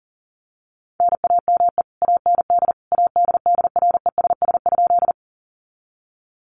Morse code, heard as a keyed tone, about 40 wpm, 700 Hz. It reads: NAME AND ADDRESS?
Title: Sentences from Top 1000 Words 40wpm